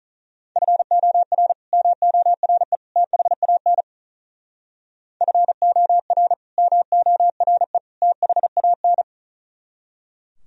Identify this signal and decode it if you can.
Morse code, heard as a keyed tone, about 46 wpm, 700 Hz. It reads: FOR MORE THAN FOR MORE THAN